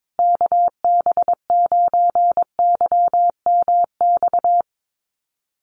Morse code, heard as a keyed tone, about 22 wpm, 700 Hz. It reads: KB9YMX
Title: USA Call Signs Repeated 3x 22wpm